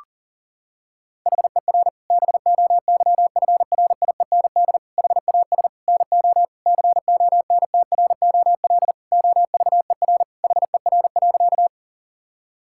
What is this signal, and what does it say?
Morse code, heard as a keyed tone, about 40 wpm, 700 Hz. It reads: HER BOYFRIEND HAS NO CONTROL OVER HER.
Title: Sentences from Top 1000 Words - Review Part 5 - 40wpm